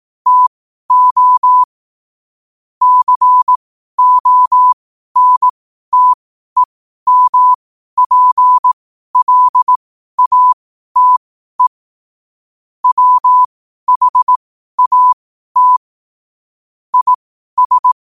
Morse code, heard as a keyed tone, about 18 wpm, spaced as for 12 wpm, 1000 Hz. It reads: TO CONTEMPLATE WHAT IS
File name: qotdfiles / 12wpm